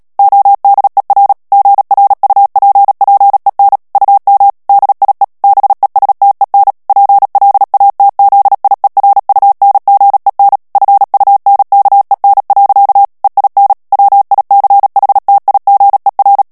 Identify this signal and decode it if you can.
Morse code, heard as a keyed tone, about 37 wpm, 800 Hz. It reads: ODER GRUPPEN UM DIE BESTEN PLATZIERUNGEN FUNKEN. EIN WICHTIGER
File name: morse mp3 / ALLEMAND texte clair 2 vitesse 35 mots minute